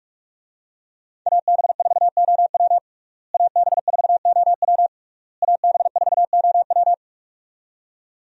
Morse code, heard as a keyed tone, about 42 wpm, 700 Hz. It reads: AB4OW AB4OW AB4OW
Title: USA Call Signs - 5 Characters - Repeated 3x - 45wpm